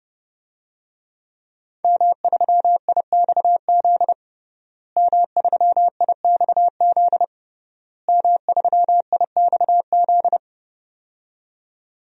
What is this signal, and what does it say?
Morse code, heard as a keyed tone, about 32 wpm, 700 Hz. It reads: M3IXZ M3IXZ M3IXZ
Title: DXCC-100 Call Signs - 5 Characters - Repeated 3x - 30wpm